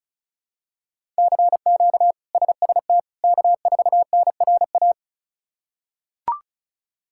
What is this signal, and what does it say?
Morse code, heard as a keyed tone, about 36 wpm, 700 Hz. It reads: CQ SST K4NRA
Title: K1USN SST QSO Practice - 35wpm